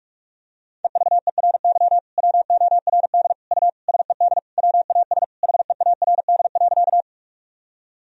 Morse code, heard as a keyed tone, about 46 wpm, 700 Hz. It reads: EVERY WORD U SED WAS HEARD.
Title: Sentences from Top 400 Words and Abbreviations 45wpm